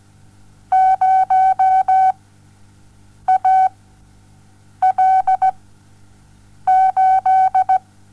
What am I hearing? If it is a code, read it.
Morse code, heard as a keyed tone, about 17 wpm, 760 Hz. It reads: 0 A L 8
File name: Onlinelehrgang / morsen / DJ4UF Morsen07